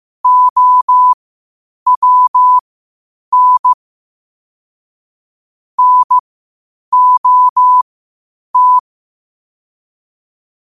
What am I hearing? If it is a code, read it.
Morse code, heard as a keyed tone, about 15 wpm, spaced as for 8 wpm, 1000 Hz. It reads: OWN NOT
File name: qotdfiles / 8wpm